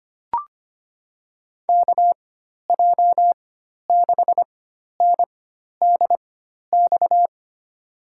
Morse code, heard as a keyed tone, about 25 wpm, spaced as for 12 wpm, 700 Hz. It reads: KJ6NDX
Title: USA Call Signs - 4x Character Spacing - 25wpm